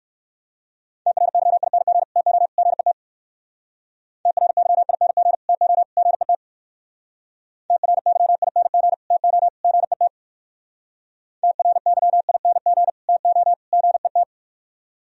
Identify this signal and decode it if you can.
Morse code, heard as a keyed tone, about 62 wpm, 700 Hz. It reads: TRYING TO GET TRYING TO GET TRYING TO GET TRYING TO GET